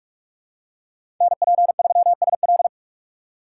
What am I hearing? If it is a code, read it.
Morse code, heard as a keyed tone, about 43 wpm, 700 Hz. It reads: NP3SL
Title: USA Call Signs - 5 Characters - 6x Character Spacing - 45wpm